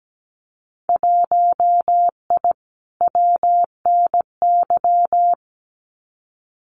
Morse code, heard as a keyed tone, about 17 wpm, 700 Hz. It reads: 1I WNY
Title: Winter Field Day Exchanges Encore 17wpm